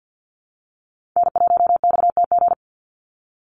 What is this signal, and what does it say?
Morse code, heard as a keyed tone, about 48 wpm, 700 Hz. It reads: N1XTG
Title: USA Call Signs 50wpm